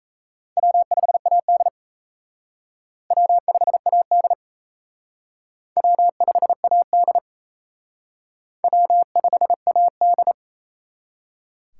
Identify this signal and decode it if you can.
Morse code, heard as a keyed tone, about 34 wpm, 700 Hz. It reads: W5AD W5AD W5AD W5AD
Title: Speed Racing - USA Call Signs 28wpm